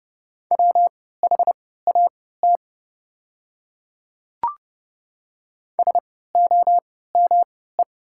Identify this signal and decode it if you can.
Morse code, heard as a keyed tone, about 30 wpm, spaced as for 17 wpm, 700 Hz. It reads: WHAT SOME
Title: 3x Character Spacing - Top 100 Words - 30wpm